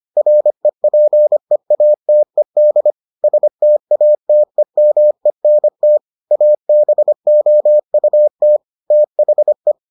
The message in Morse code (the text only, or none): REPEATED STATEMENT ABOUT THE